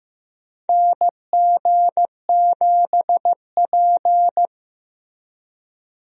Morse code, heard as a keyed tone, about 15 wpm, 700 Hz. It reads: NG7P
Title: USA Call Signs - 4 Characters - Repeated 2x - 15wpm